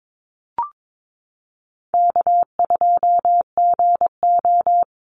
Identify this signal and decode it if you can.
Morse code, heard as a keyed tone, about 22 wpm, 700 Hz. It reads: K2GO
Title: Easy Call Signs Repeated 3x - 22wpm